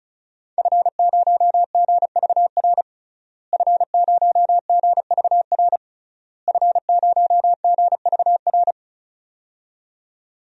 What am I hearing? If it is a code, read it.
Morse code, heard as a keyed tone, about 36 wpm, 700 Hz. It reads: F0GVR F0GVR F0GVR
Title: DXCC-100 Call Signs - 5 Characters - Repeated 3x - 35wpm